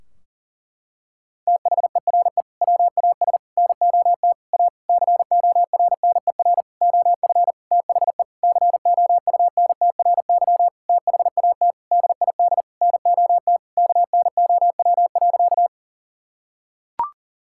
Morse code, heard as a keyed tone, about 37 wpm, 700 Hz. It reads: THERE WAS NOT A CORNER OF THE COUNTRY THAT DID NOT KNOW.
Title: Sentences from Top 1600 Words 40wpm